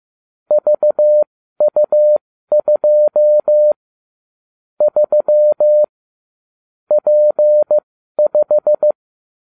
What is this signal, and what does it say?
Morse code, heard as a keyed tone, about 15 wpm, spaced as for 12 wpm, 600 Hz. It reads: VU2 3 P5